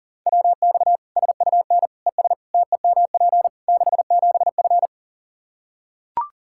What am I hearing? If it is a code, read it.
Morse code, heard as a keyed tone, about 40 wpm, 700 Hz. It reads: WX SUN ES TEMP 67F
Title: Practice QSO - Exchange 2 - Info - 40wpm